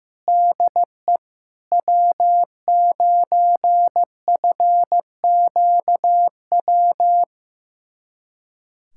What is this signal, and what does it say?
Morse code, heard as a keyed tone, about 15 wpm, 700 Hz. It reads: DE W9FQW
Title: USA Callsign DE Callsign - 5 Characters - 15wpm